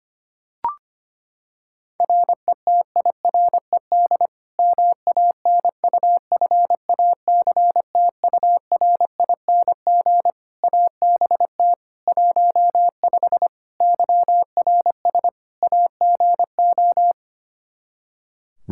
Morse code, heard as a keyed tone, about 25 wpm, 700 Hz. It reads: RETIRED MANUFACTURING ABT 15 YRS AGO
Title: Practice QSO - Exchange 3 - More Info - 25wpm